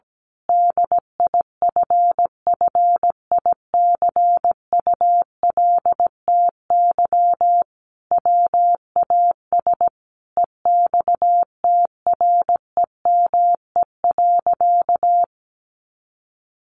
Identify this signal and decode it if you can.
Morse code, heard as a keyed tone, about 17 wpm, 700 Hz. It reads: DIFFICULTY WAS EXTREME.
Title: Sentences from Top 1500 Words 17wpm